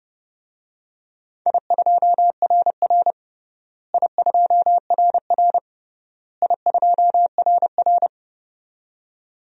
Morse code, heard as a keyed tone, about 29 wpm, 700 Hz. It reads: I2RR I2RR I2RR
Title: DXCC-100 Call Signs - 4 Characters - Repeated 3x - 30wpm